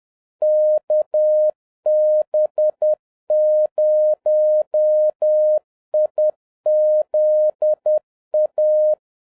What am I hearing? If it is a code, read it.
Morse code, heard as a keyed tone, about 10 wpm, 610 Hz. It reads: KB0IZA